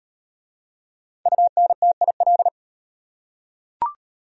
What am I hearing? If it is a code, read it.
Morse code, heard as a keyed tone, about 37 wpm, 700 Hz. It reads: UNTIL